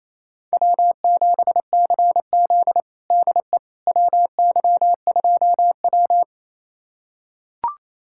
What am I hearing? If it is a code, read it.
Morse code, heard as a keyed tone, about 28 wpm, 700 Hz. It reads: W7CZ DE WY2W
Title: USA Callsign DE Callsign - 4 Characters - 28wpm